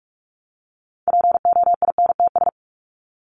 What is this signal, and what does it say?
Morse code, heard as a keyed tone, about 44 wpm, 700 Hz. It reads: POINTS
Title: ES:S Suffix - Top 100 Words with Base Words 45wpm